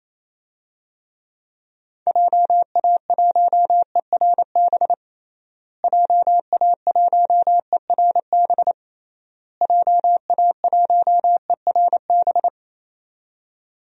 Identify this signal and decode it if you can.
Morse code, heard as a keyed tone, about 28 wpm, 700 Hz. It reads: JA1ERB JA1ERB JA1ERB
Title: DXCC-100 Call Signs - 6 Characters - Repeated 3x - 28wpm